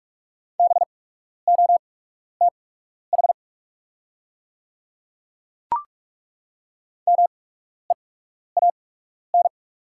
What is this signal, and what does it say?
Morse code, heard as a keyed tone, about 43 wpm, spaced as for 12 wpm, 690 Hz. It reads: BOTH MEAN
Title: 8x Character Spacing - Top 500 Words - 45wpm